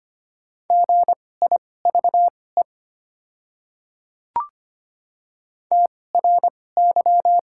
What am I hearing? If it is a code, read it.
Morse code, heard as a keyed tone, about 25 wpm, spaced as for 18 wpm, 700 Hz. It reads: GIVE TRY
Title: 2x Character Spacing - Top 300 Words - 25wpm